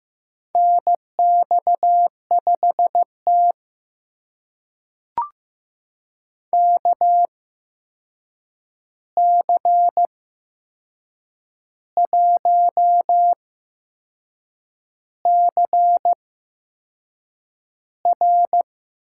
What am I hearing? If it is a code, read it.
Morse code, heard as a keyed tone, about 15 wpm, spaced as for 6 wpm, 700 Hz. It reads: NX5T KC1CR
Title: USA Call Signs - 8x Character Spacing - 15wpm